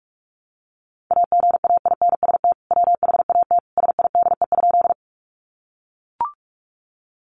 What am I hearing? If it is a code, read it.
Morse code, heard as a keyed tone, about 44 wpm, 700 Hz. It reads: AGAINST WHAT SIDE?
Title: Sentences from Top 400 Words 45wpm